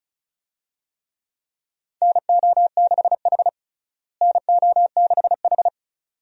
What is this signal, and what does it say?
Morse code, heard as a keyed tone, about 36 wpm, 700 Hz. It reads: NO6H NO6H
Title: USA Call Signs - 4 Characters - Repeated 2x - 35wpm